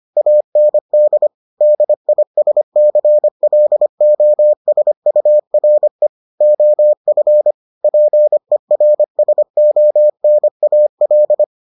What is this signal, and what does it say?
Morse code, heard as a keyed tone, about 25 wpm, 600 Hz. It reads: AND DISCLOSURE OF PERSONAL